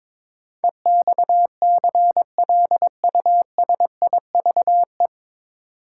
Morse code, heard as a keyed tone, about 22 wpm, 700 Hz. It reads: EXCLUSIVE
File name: 9 Letter Words - First 3 Letters Given - 22wpm